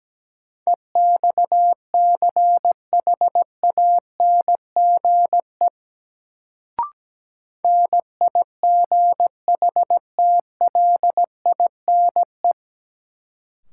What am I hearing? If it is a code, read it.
Morse code, heard as a keyed tone, about 17 wpm, 700 Hz. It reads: EXCHANGE NIGHTLINE
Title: Sets of 5 Words - New Word Drill - 17wpm